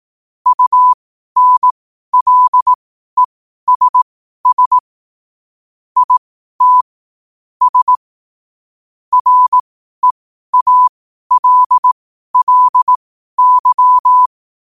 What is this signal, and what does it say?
Morse code, heard as a keyed tone, about 18 wpm, spaced as for 12 wpm, 1000 Hz. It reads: UNLESS IT S REALLY
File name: qotdfiles / 12wpm